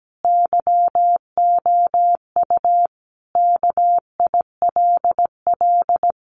Morse code, heard as a keyed tone, about 17 wpm, 700 Hz. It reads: YOU KILL
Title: Sentences from Top 1400 Words 17wpm